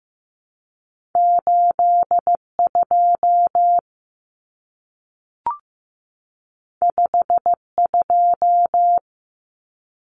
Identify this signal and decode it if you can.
Morse code, heard as a keyed tone, about 15 wpm, 700 Hz. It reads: 82 52
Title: Common Cut Numbers - 2-Digit 15wpm